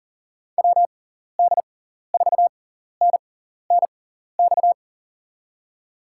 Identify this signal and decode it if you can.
Morse code, heard as a keyed tone, about 40 wpm, spaced as for 14 wpm, 700 Hz. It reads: WD4NNX
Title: USA Call Signs - 6x Character Spacing - 40wpm